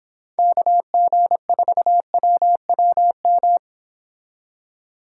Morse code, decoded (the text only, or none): KG4WWM